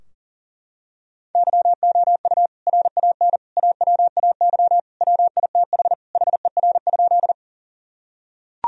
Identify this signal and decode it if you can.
Morse code, heard as a keyed tone, about 40 wpm, 700 Hz. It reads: YOU RAN AWAY WITH HER?
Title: Sentences from Top 1000 Words - Review Part 3 - 40wpm